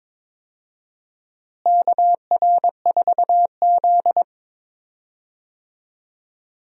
Morse code, heard as a keyed tone, about 22 wpm, 700 Hz. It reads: KR4Z
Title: USA Call Signs - 4 Characters - Repeated 3x - 22wpm